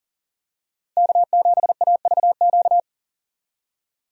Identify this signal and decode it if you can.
Morse code, heard as a keyed tone, about 39 wpm, 700 Hz. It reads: K7AVQ
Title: USA Call Signs - 5 Characters - 40wpm